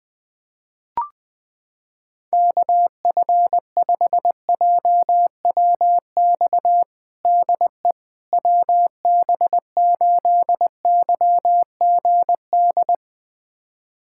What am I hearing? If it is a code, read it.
Morse code, heard as a keyed tone, about 20 wpm, 700 Hz. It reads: KF5JWX DE WB8YGD
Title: USA Callsign DE Callsign 20wpm